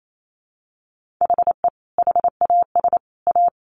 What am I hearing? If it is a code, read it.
Morse code, heard as a keyed tone, about 28 wpm, 700 Hz. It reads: HE HAS A
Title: Sentences from Top 1400 Words 28wpm